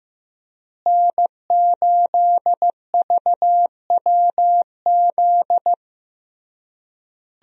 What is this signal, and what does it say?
Morse code, heard as a keyed tone, about 16 wpm, 700 Hz. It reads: N8VWZ
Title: USA Call Signs - 5 Characters - Repeated 3x - 15wpm